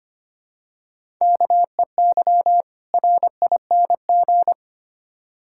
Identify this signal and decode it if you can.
Morse code, heard as a keyed tone, about 25 wpm, 700 Hz. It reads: KEY RING
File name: practice / 25wpm / Difficult Words with Hints - 25wpm